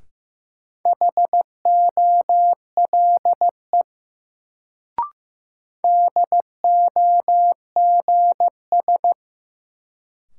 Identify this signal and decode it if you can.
Morse code, heard as a keyed tone, about 15 wpm, 700 Hz. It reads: HOLE DOGS
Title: Sentences from Top 1600 Words - New Word Drill - 15wpm